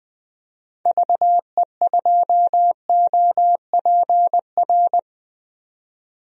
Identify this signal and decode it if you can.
Morse code, heard as a keyed tone, about 20 wpm, 700 Hz. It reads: VE2OPR
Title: DXCC-100 Call Signs - 6 Characters - 20wpm